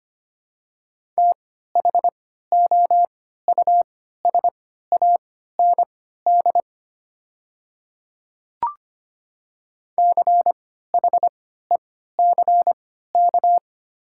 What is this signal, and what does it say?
Morse code, heard as a keyed tone, about 25 wpm, spaced as for 14 wpm, 700 Hz. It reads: THOUSAND CHECK
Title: 3x Character Spacing - Top 500 Words - 25wpm